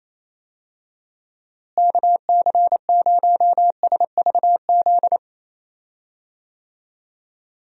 Morse code, heard as a keyed tone, about 28 wpm, 700 Hz. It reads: KC0SVZ
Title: USA Call Signs - 6 Characters - Repeated 3x - 28wpm